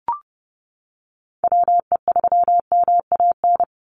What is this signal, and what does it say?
Morse code, heard as a keyed tone, about 30 wpm, 700 Hz. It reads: WE3MAN
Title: USA Call Signs 30wpm